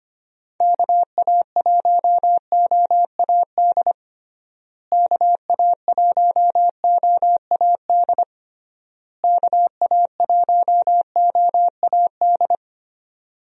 Morse code, heard as a keyed tone, about 25 wpm, 700 Hz. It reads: KA1OAD KA1OAD KA1OAD
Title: USA Call Signs - 6 Characters - Repeated 3x - 25wpm